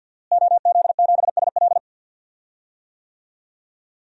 Morse code, heard as a keyed tone, about 48 wpm, 700 Hz. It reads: OZ7SL
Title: DXCC-100 Call Signs - 5 Characters - Repeated 2x - 50wpm